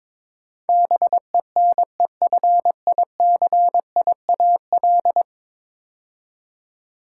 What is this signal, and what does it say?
Morse code, heard as a keyed tone, about 22 wpm, 700 Hz. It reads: BENEFICIAL
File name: practice / 22wpm / 10 Letter Words - First 3 Letters Given - 22wpm